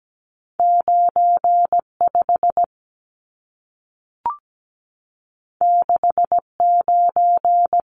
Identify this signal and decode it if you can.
Morse code, heard as a keyed tone, about 17 wpm, 700 Hz. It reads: 95 69
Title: Numbers - 2-Digit 17wpm